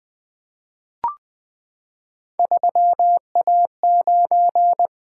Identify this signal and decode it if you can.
Morse code, heard as a keyed tone, about 20 wpm, 700 Hz. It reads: 3A9
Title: Three Letter-Number Combinations - 20wpm